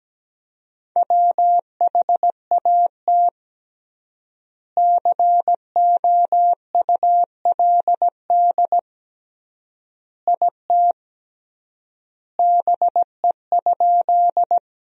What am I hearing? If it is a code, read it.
Morse code, heard as a keyed tone, about 17 wpm, 700 Hz. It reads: WHAT COULD IT BE?
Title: 3x Word-Spacing - Sentences from Top 100 Words - 17wpm